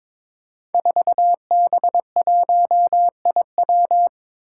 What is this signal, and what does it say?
Morse code, heard as a keyed tone, about 23 wpm, 700 Hz. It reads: 4B1IW
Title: Five Challenging Letter-Number Combinations 22wpm